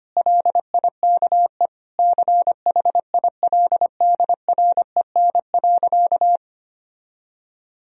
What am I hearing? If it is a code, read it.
Morse code, heard as a keyed tone, about 25 wpm, 700 Hz. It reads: LIKE CHILDREN.
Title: Sentences from Top 1000 Words - Review Part 2 - 25wpm